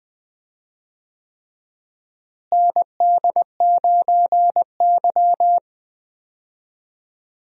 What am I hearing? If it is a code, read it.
Morse code, heard as a keyed tone, about 20 wpm, 700 Hz. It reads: ND9Y
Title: USA Call Signs - 4 Characters - Repeated 3x - 20wpm